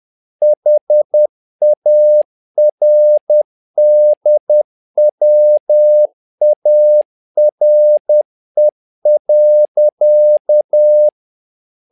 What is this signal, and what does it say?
Morse code, heard as a keyed tone, about 10 wpm, 600 Hz. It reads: HARDWARE.
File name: learning center / tower trouble / tower trouble 10wpm